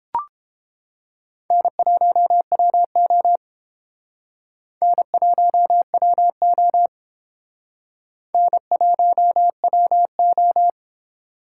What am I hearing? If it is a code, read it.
Morse code, heard as a keyed tone, about 31 wpm, 700 Hz. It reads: N1WO N1WO N1WO